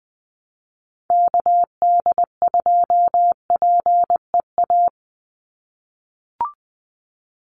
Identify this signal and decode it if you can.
Morse code, heard as a keyed tone, about 20 wpm, 700 Hz. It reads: KD2PEA
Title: USA Call Signs 20wpm